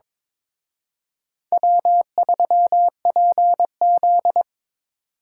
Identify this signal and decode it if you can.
Morse code, heard as a keyed tone, about 26 wpm, 700 Hz. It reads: W3PZ
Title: Speed Racing - USA 1x2 Call Signs 22wpm